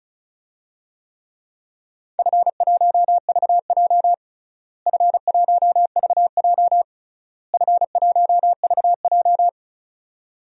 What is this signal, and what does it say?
Morse code, heard as a keyed tone, about 34 wpm, 700 Hz. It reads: F1VJ F1VJ F1VJ
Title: DXCC-100 Call Signs - 4 Characters - Repeated 3x - 35wpm